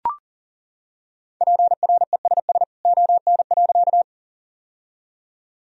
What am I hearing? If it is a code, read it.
Morse code, heard as a keyed tone, about 40 wpm, 700 Hz. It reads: PRESS ON.
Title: Sentences from Top 1000 Words - Review Part 1 - 40wpm